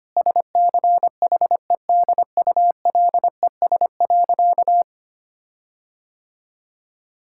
Morse code, heard as a keyed tone, about 25 wpm, 700 Hz. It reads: SCHEDULES.
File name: practice / 25wpm / Difficult Sentences with Hints - 25wpm